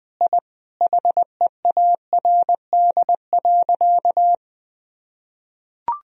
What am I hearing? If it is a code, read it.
Morse code, heard as a keyed tone, about 20 wpm, 700 Hz. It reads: I HEARD.
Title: Sentences from Top 1000 Words - Review Part 4 - 20wpm